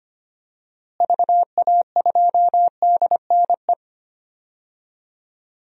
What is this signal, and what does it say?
Morse code, heard as a keyed tone, about 25 wpm, 700 Hz. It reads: VA2DNE
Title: DXCC-100 Call Signs - 6 Characters - Repeated 2x - 25wpm